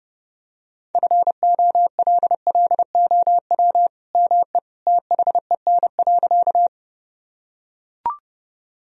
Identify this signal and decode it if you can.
Morse code, heard as a keyed tone, about 30 wpm, 700 Hz. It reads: FOLLOW ME THEN.
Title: Sentences from Top 1000 Words - Review Part 1 - 30wpm